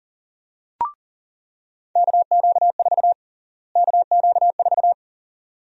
Morse code, heard as a keyed tone, about 37 wpm, 700 Hz. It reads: KQ4 KQ4
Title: USA Call Sign Prefixes - Repeated 2x - 40wpm